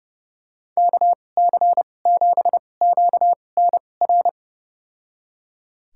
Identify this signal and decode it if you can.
Morse code, heard as a keyed tone, about 31 wpm, spaced as for 22 wpm, 700 Hz. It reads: KC7QNR
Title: USA Call Signs - 6 Characters - 2x Character Spacing - 30wpm